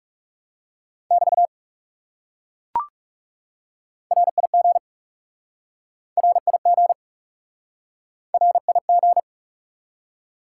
Morse code, heard as a keyed tone, about 37 wpm, 700 Hz. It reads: <BT> RIG RIG RIG
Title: Speed-Racing Top 180 QSO Elements 30wpm